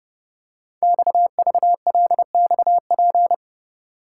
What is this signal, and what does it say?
Morse code, heard as a keyed tone, about 28 wpm, 700 Hz. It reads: XVLXP
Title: Code Groups - 5 Letters - 30wpm